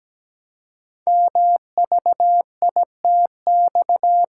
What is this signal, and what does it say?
Morse code, heard as a keyed tone, about 17 wpm, 700 Hz. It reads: MVITX
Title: Code Groups - 5 Letters - 17wpm